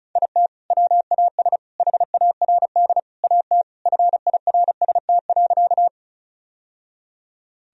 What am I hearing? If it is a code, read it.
Morse code, heard as a keyed tone, about 36 wpm, 700 Hz. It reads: IT WAS HARD AT FIRST.
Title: Sentences from Top 1000 Words - Review Part 1 - 35wpm